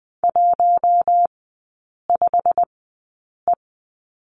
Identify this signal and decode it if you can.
Morse code, heard as a keyed tone, about 20 wpm, spaced as for 8 wpm, 700 Hz. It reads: 15E
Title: 138 Lesson 23e - Letter f - Three Characters 20wpm